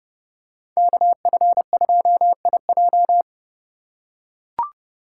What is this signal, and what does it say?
Morse code, heard as a keyed tone, about 29 wpm, 700 Hz. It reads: KF2IJ